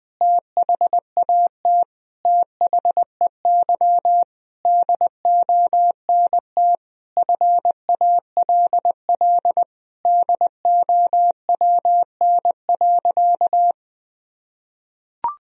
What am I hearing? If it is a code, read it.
Morse code, heard as a keyed tone, about 20 wpm, 700 Hz. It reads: THAT THEY DONT FALL DOWN.
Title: Sentences from Top 1000 Words - Review Part 2 - 20wpm